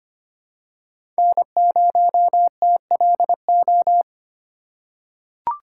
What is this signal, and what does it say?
Morse code, heard as a keyed tone, about 25 wpm, 700 Hz. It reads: N0TLO
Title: USA Call Signs - 5 Characters - 3x Character Spacing - 25wpm